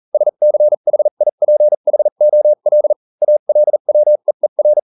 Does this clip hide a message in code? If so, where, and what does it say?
Yes: Morse code from 0.1 to 4.8 s, heard as a keyed tone, about 40 wpm, spaced as for 35 wpm, 600 Hz. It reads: SCHIPHOL ALWEER